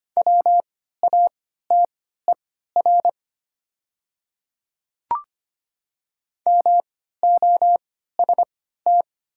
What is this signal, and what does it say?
Morse code, heard as a keyed tone, about 25 wpm, spaced as for 14 wpm, 700 Hz. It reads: WATER MOST
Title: 3x Character Spacing - Top 100 Words - 25wpm